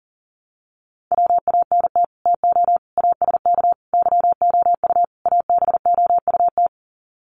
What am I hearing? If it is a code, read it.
Morse code, heard as a keyed tone, about 40 wpm, 700 Hz. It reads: WANT TO ASK YOU ABOUT
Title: Sets of 5 Words Encore2 40wpm